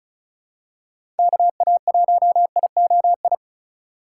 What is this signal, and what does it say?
Morse code, heard as a keyed tone, about 36 wpm, 700 Hz. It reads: KA1IOI
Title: USA Call Signs - 6 Characters - 2x Character Spacing - 35wpm